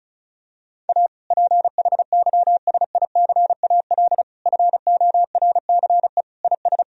Morse code, heard as a keyed tone, about 36 wpm, 700 Hz. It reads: A PHYSICAL FORCE IS
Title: Sentences from Top 1000 Words - Review Part 4 - 35wpm